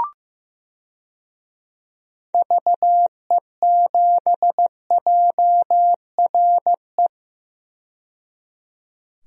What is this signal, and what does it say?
Morse code, heard as a keyed tone, about 15 wpm, 700 Hz. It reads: VE7JRE
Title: DXCC-100 Call Signs - 6 Characters - 15wpm